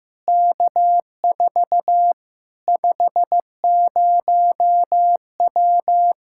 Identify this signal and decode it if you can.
Morse code, heard as a keyed tone, about 15 wpm, 700 Hz. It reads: K4 50W
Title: Practice QSO - Exchange 2 - Info - 15wpm